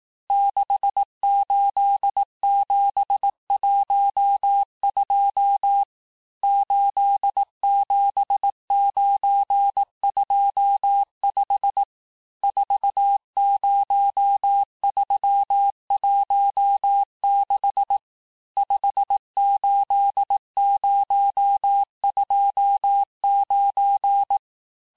Morse code, heard as a keyed tone, about 18 wpm, 800 Hz. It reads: 68712 87925 40316 58029
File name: cours-du-lundi files / cours-1588-texte-chiffre-vitesse-17MPM